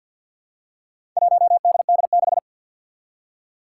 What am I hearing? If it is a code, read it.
Morse code, heard as a keyed tone, about 47 wpm, 700 Hz. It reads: 1DD6
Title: Four Challenging Letter-Number Combinations 50wpm